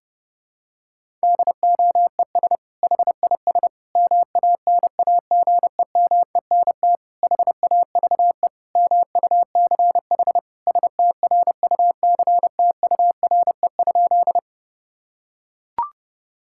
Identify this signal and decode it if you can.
Morse code, heard as a keyed tone, about 30 wpm, 700 Hz. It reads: DOES HIS MANAGEMENT HAVE MUCH STRUCTURE?
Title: Sentences from Top 1000 Words - Review Part 6 - 30wpm